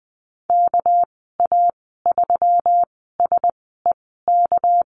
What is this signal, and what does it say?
Morse code, heard as a keyed tone, about 20 wpm, spaced as for 14 wpm, 700 Hz. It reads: KA3SEK
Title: 213 Lesson 34g - Number 8 - Call Signs 20wpm